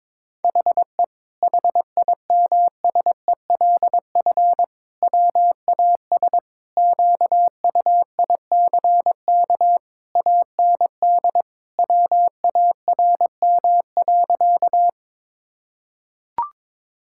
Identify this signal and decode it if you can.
Morse code, heard as a keyed tone, about 22 wpm, 700 Hz. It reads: HE HIMSELF WAS QUICK AND WARM.
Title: Sentences from Top 1600 Words 22wpm